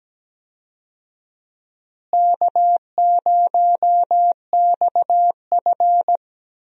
Morse code, heard as a keyed tone, about 17 wpm, 700 Hz. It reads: K0XF
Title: USA Call Signs - 4 Characters - Repeated 2x - 17wpm